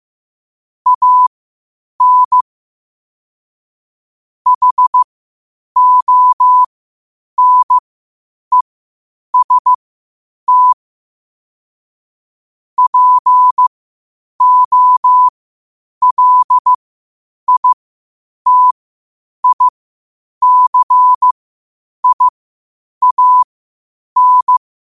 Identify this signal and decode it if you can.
Morse code, heard as a keyed tone, about 15 wpm, spaced as for 8 wpm, 1000 Hz. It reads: AN HONEST POLITICIAN
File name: qotdfiles / 8wpm